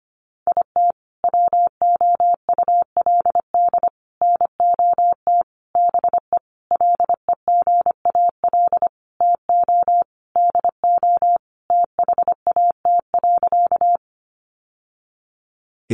Morse code, heard as a keyed tone, about 25 wpm, 700 Hz. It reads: IT WOULD NOT BE LEGAL TO DO THAT.
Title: Sentences from Top 1200 Words 25wpm